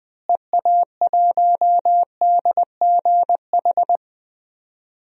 Morse code, heard as a keyed tone, about 21 wpm, 700 Hz. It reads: EA1DGH
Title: DXCC-100 Call Signs - 6 Characters - Repeated 3x - 20wpm